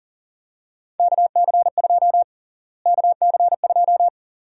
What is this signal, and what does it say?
Morse code, heard as a keyed tone, about 40 wpm, 700 Hz. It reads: KC2 KC2
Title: USA Call Sign Prefixes - Repeated 2x - 40wpm